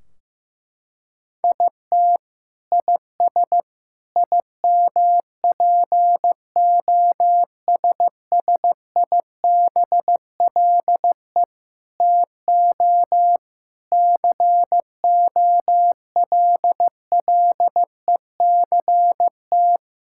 Morse code, heard as a keyed tone, about 15 wpm, 700 Hz. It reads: IT IS IMPOSSIBLE TO COLLECT
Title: Sentences from Top 1600 Words 15wpm